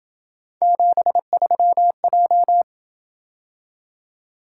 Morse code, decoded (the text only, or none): 73J